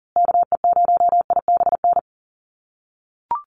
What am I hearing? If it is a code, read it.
Morse code, heard as a keyed tone, about 40 wpm, 700 Hz. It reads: KE0IBN
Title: USA Call Signs Repeated 3x 40wpm